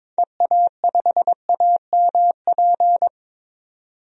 Morse code, heard as a keyed tone, about 21 wpm, 700 Hz. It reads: EA5AMP
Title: DXCC-100 Call Signs - 6 Characters - 22wpm